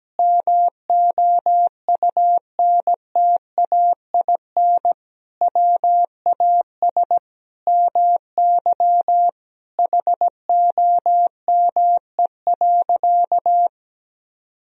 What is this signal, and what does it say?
Morse code, heard as a keyed tone, about 17 wpm, 700 Hz. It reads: MOUNTAIN WAS MY HOME.
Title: Sentences from Top 1000 Words - Review Part 1 - 17wpm